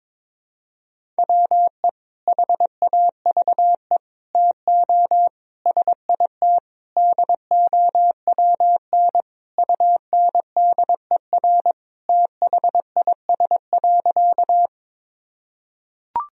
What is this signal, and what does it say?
Morse code, heard as a keyed tone, about 22 wpm, 700 Hz. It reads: WE HAVE TO SIT DOWN UNDER THIS.
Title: Sentences from Top 1000 Words - Review Part 1 - 22wpm